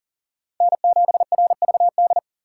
Morse code, heard as a keyed tone, about 37 wpm, 700 Hz. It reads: N7RVD
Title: K1USN SST QSO Practice - 40wpm